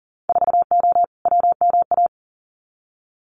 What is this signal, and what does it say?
Morse code, heard as a keyed tone, about 43 wpm, 700 Hz. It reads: 4O WMA
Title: Winter Field Day Exchanges Encore 40wpm